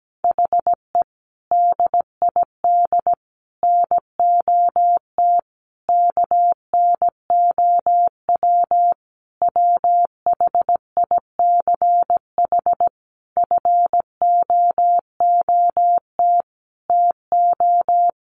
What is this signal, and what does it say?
Morse code, heard as a keyed tone, about 17 wpm, 700 Hz. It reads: HE DID NOT KNOW WHICH FOOT TO
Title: Sentences from Top 1300 Words 17wpm